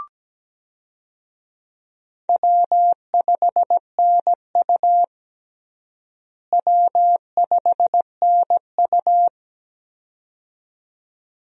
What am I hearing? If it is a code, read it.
Morse code, heard as a keyed tone, about 17 wpm, 700 Hz. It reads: W5NU W5NU
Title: USA Call Signs - 4 Characters - Repeated 2x - 17wpm